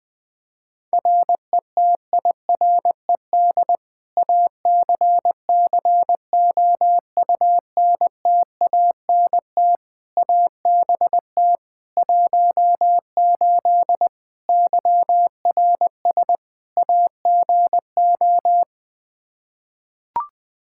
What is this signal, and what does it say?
Morse code, heard as a keyed tone, about 20 wpm, 700 Hz. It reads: RETIRED ACCOUNTANT ABT 18 YRS AGO
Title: K1USN SST QSO Practice 20wpm